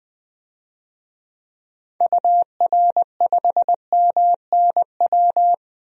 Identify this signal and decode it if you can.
Morse code, heard as a keyed tone, about 20 wpm, 700 Hz. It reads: UR5MNW
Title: DXCC-100 Call Signs - 6 Characters - 20wpm